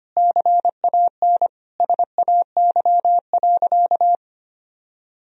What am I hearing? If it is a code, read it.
Morse code, heard as a keyed tone, about 25 wpm, 700 Hz. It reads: CAN SAY.
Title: Sentences from Top 1000 Words - Review Part 2 - 25wpm